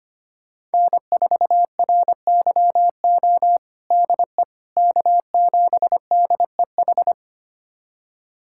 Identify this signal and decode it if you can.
Morse code, heard as a keyed tone, about 25 wpm, 700 Hz. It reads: N4RYO DE K7DEH
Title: USA Callsign DE Callsign - 5 Characters - 25wpm